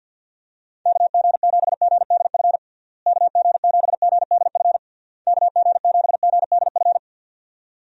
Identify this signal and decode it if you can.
Morse code, heard as a keyed tone, about 50 wpm, 700 Hz. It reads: KG7GDF KG7GDF KG7GDF
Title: USA Call Signs - 6 Characters - Repeated 3x - 50wpm